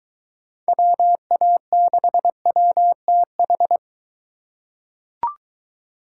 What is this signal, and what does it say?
Morse code, heard as a keyed tone, about 23 wpm, 700 Hz. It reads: WA6WTH